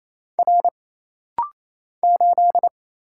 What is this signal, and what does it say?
Morse code, heard as a keyed tone, about 27 wpm, 700 Hz. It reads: R 8
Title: Single Letter-Number - Rapid-Fire 28wpm